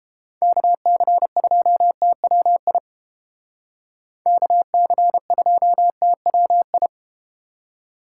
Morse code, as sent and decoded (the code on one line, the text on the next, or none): -.- -.-. ..--- - .-- .. / -.- -.-. ..--- - .-- ..
KC2TWI KC2TWI